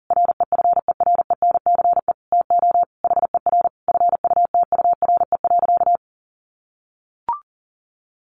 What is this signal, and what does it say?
Morse code, heard as a keyed tone, about 40 wpm, 700 Hz. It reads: REFERENCE TO HER FUTURE.
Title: Sentences from Top 1200 Words 40wpm